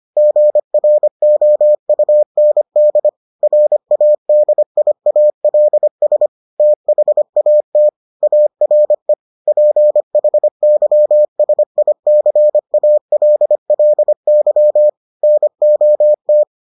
Morse code, heard as a keyed tone, about 25 wpm, 600 Hz. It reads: GROUND RADIALS THAT ARE PHYSICALLY NOT